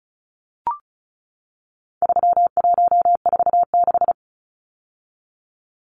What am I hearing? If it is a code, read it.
Morse code, heard as a keyed tone, about 34 wpm, 700 Hz. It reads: 3146
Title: Numbers - 4-Digit 35wpm